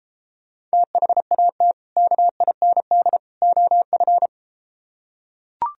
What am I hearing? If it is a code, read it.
Morse code, heard as a keyed tone, about 33 wpm, 700 Hz. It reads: THAT KIND OF